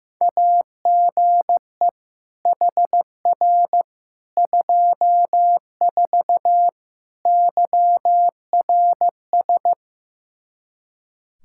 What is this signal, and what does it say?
Morse code, heard as a keyed tone, about 15 wpm, 700 Hz. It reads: AGE HR 24 YRS
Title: K1USN SST QSO Practice 15wpm